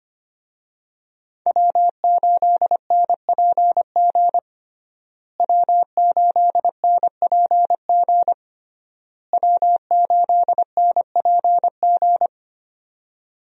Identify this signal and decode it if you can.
Morse code, heard as a keyed tone, about 25 wpm, 700 Hz. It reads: W8NPG W8NPG W8NPG
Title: USA Call Signs - 5 Characters - Repeated 3x - 25wpm